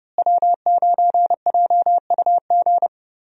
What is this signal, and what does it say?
Morse code, heard as a keyed tone, about 29 wpm, 700 Hz. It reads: W9JUG
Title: USA Call Signs - 5 Characters - Repeated 3x - 30wpm